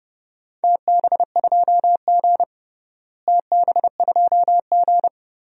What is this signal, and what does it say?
Morse code, heard as a keyed tone, about 30 wpm, 700 Hz. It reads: TB2G TB2G
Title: DXCC-100 Call Signs - 4 Characters - Repeated 2x - 30wpm